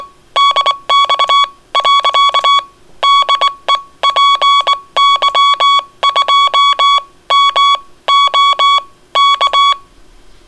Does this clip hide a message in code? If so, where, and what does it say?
Yes: Morse code from 0.4 to 9.7 s, heard as a keyed tone, about 22 wpm, spaced as for 17 wpm, 1120 Hz. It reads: DX . DE PY2MOK